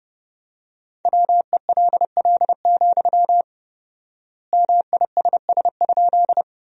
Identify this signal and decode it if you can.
Morse code, heard as a keyed tone, about 30 wpm, 700 Hz. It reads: WELL, MISS?
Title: 4x Word-Spacing - Sentences from Top 500 Words - 30wpm